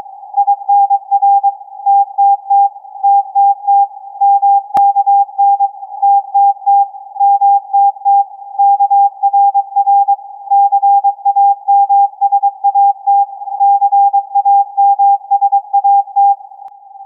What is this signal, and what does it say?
Morse code, heard as a keyed tone, about 22 wpm, 800 Hz. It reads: INR TTT TTT MKN TTT MTT KRR CAMSAT CAMSAT
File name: CAS4A-TURN-AIRSPY-01